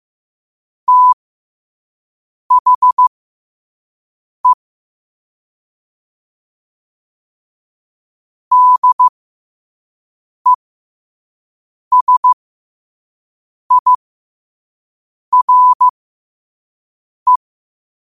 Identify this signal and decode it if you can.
Morse code, heard as a keyed tone, about 15 wpm, spaced as for 5 wpm, 1000 Hz. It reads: THE DESIRE
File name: qotdfiles / 5wpm